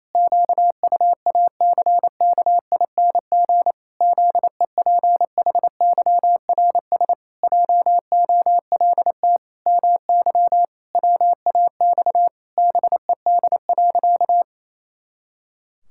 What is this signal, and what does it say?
Morse code, heard as a keyed tone, about 28 wpm, 700 Hz. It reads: QUACKING ZEPHYRS JOLT MY WAX BED.
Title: Pangram Challenge v2 - 28wpm